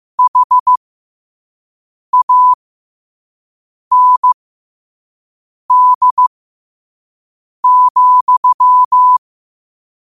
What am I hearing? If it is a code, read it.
Morse code, heard as a keyed tone, about 15 wpm, spaced as for 5 wpm, 1000 Hz. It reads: HAND,
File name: qotdfiles / 5wpm